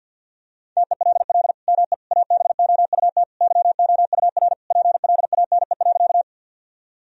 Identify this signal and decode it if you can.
Morse code, heard as a keyed tone, about 50 wpm, 700 Hz. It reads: TELL ME ABOUT YOUR PLANE.
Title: Sentences from Top 1000 Words - Review Part 3 - 50wpm